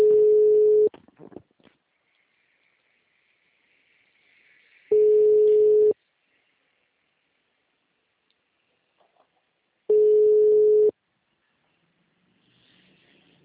Ringback tone (Europe, 425 Hz).
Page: Campo Limpo - SP